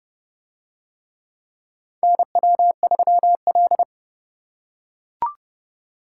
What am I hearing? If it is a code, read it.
Morse code, heard as a keyed tone, about 29 wpm, 700 Hz. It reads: NW3L